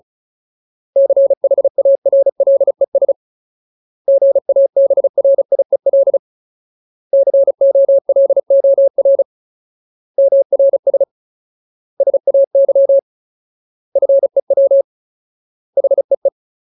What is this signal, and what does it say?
Morse code, heard as a keyed tone, about 34 wpm, 550 Hz. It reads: CHARLES GABRIEL COLOR MRS SAY FEW HEE
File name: Ultimate Word List / 35WPM Ultimate